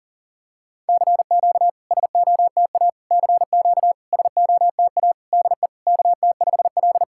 Morse code, heard as a keyed tone, about 44 wpm, 700 Hz. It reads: CQ SOTA CQ SOTA DE KT5L
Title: SOTA QSO Practice 40wpm